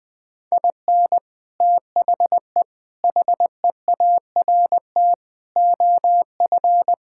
Morse code, heard as a keyed tone, about 20 wpm, 700 Hz. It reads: IN THE HEART OF
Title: Sentences with 5-Word Phrases - 20wpm